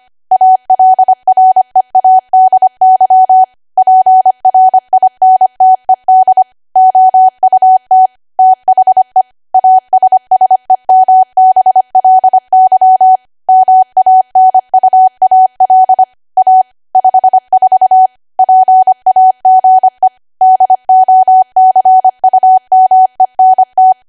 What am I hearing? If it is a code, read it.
Morse code, heard as a keyed tone, about 25 wpm, 750 Hz. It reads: ALREADY PRINTED OUT THE ASSEMBLY MANUAL A 54 PAGE DOCUMENT